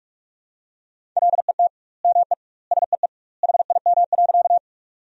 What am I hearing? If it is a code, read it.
Morse code, heard as a keyed tone, about 46 wpm, 700 Hz. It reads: LET ME SEE HIM.
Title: 2x Word-Spacing - Sentences from Top 300 Words - 45wpm